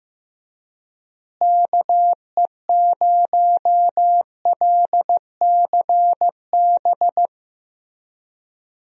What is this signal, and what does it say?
Morse code, heard as a keyed tone, about 15 wpm, 700 Hz. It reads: KE0LCB
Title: USA Call Signs - 6 Characters - Repeated 3x - 15wpm